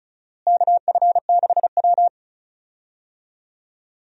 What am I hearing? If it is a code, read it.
Morse code, heard as a keyed tone, about 36 wpm, 700 Hz. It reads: KF6W
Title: USA Call Signs - 4 Characters - Repeated 2x - 35wpm